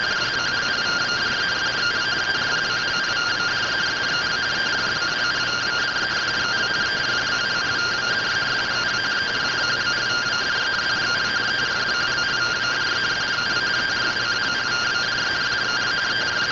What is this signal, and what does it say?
Two-tone teleprinter signal, about 170 Hz shift.
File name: findings / SAGITAL fr15 nb203 3mm.gen-34 rtty rx